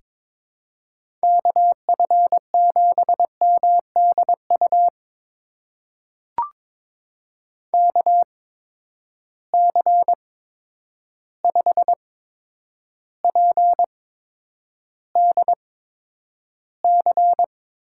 Morse code, heard as a keyed tone, about 22 wpm, spaced as for 9 wpm, 700 Hz. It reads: KF7MDU KC5PDC
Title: USA Call Signs - 6 Characters - 8x Character Spacing - 22wpm